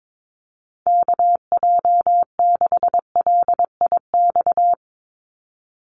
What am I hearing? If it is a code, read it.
Morse code, heard as a keyed tone, about 22 wpm, 700 Hz. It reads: KJ6LIX
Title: USA Call Signs 22wpm